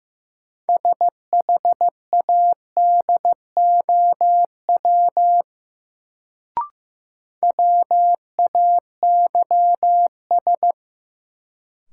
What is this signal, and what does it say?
Morse code, heard as a keyed tone, about 15 wpm, 700 Hz. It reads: SHADOW WAYS
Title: Sets of 5 Words Encore - New Word Drill - 15wpm